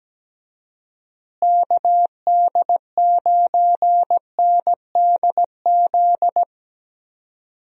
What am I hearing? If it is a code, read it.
Morse code, heard as a keyed tone, about 17 wpm, 700 Hz. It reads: KD9NDZ
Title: USA Call Signs - 6 Characters - Repeated 3x - 17wpm